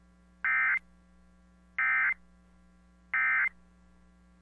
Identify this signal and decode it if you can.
Emergency Alert System (SAME) tones, heard as harsh buzzing bursts. It is heard from 0.4 to 3.5 s.